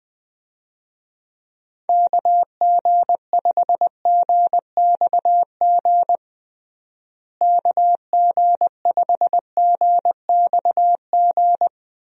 Morse code, heard as a keyed tone, about 20 wpm, 700 Hz. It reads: KG5GXG KG5GXG
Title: USA Call Signs - 6 Characters - Repeated 2x - 20wpm